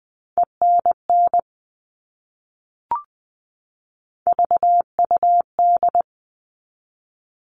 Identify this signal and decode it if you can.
Morse code, heard as a keyed tone, about 20 wpm, 700 Hz. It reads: ENN VUD
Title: All Cut Numbers - 3-Digit 20wpm